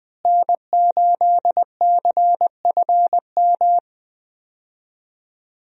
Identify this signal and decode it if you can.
Morse code, heard as a keyed tone, about 20 wpm, 700 Hz. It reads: N8CFM
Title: USA Call Signs - 5 Characters - Repeated 3x - 20wpm